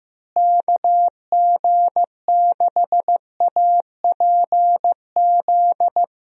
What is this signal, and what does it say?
Morse code, heard as a keyed tone, about 15 wpm, 700 Hz. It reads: KG6APZ
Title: USA Call Signs - 6 Characters - Repeated 3x - 15wpm